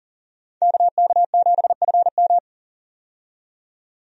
Morse code, heard as a keyed tone, about 40 wpm, 700 Hz. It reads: KK7FM
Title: USA Call Signs - 5 Characters - 40wpm